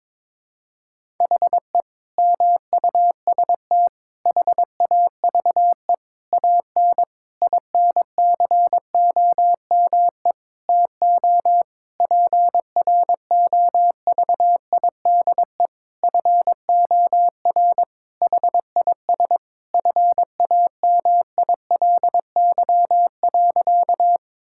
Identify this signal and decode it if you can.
Morse code, heard as a keyed tone, about 22 wpm, 700 Hz. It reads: HE MUST HAVE AN INCOME TO PROVIDE FOR HIS FAMILY.
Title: Sentences from Top 1000 Words - Review Part 5 - 22wpm